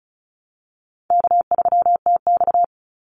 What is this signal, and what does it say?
Morse code, heard as a keyed tone, about 34 wpm, 700 Hz. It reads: K3TX
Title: Easy Call Signs Repeated 3x - 35wpm